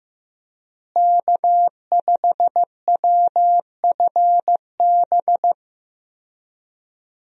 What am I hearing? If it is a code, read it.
Morse code, heard as a keyed tone, about 15 wpm, 700 Hz. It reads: K5WFB
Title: USA Call Signs - 5 Characters - Repeated 2x - 15wpm